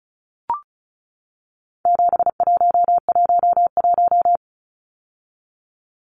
Morse code, heard as a keyed tone, about 34 wpm, 700 Hz. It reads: 7111
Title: Numbers - 4-Digit 35wpm